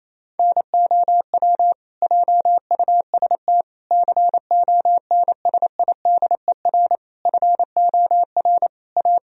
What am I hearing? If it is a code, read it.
Morse code, heard as a keyed tone, about 28 wpm, 700 Hz. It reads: NOW JUST CONSIDER FOR A
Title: Sentences from Top 1000 Words - Review Part 4 - 28wpm